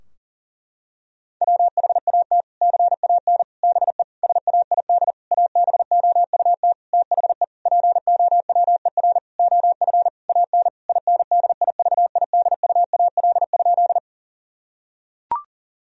Morse code, heard as a keyed tone, about 37 wpm, 700 Hz. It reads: WHAT CAN BE SAID ABOUT THE POWER OF AN INDIVIDUAL?
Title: Sentences from Top 1000 Words - Review Part 4 - 40wpm